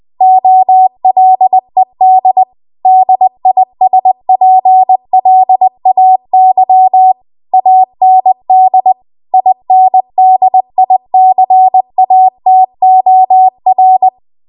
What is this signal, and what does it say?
Morse code, heard as a keyed tone, about 20 wpm, 750 Hz. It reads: OLED DISPLAY AND INDICATOR